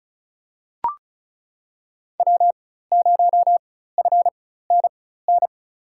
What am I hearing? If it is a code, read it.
Morse code, heard as a keyed tone, about 36 wpm, spaced as for 16 wpm, 700 Hz. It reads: W0FNN
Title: USA Call Signs - 5 Characters - 4x Character Spacing - 35wpm